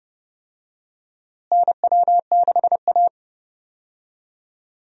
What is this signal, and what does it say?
Morse code, heard as a keyed tone, about 29 wpm, 700 Hz. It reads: NW6A
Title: USA Call Signs - 4 Characters - Repeated 2x - 30wpm